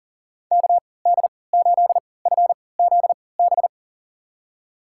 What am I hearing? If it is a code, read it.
Morse code, heard as a keyed tone, about 40 wpm, spaced as for 23 wpm, 700 Hz. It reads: KD8FZB
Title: USA Call Signs - 6 Characters - 3x Character Spacing - 40wpm